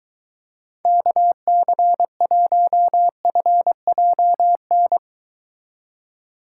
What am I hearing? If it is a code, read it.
Morse code, heard as a keyed tone, about 23 wpm, 700 Hz. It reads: KC1FJN